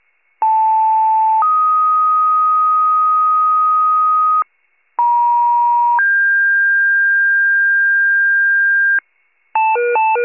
Two-tone page, heard as a one-second tone, then a three-second tone, about 880 then 1250 Hz, about 950 then 1600 Hz, with noise. It is heard from 0.4 to 9.0 s.
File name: pager / 83 RESCUE DUTY / wcalert